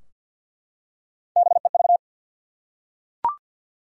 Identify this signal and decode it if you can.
Morse code, heard as a keyed tone, about 52 wpm, 700 Hz. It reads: BEV
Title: USA Common Names - Max 4 Letters - 50wpm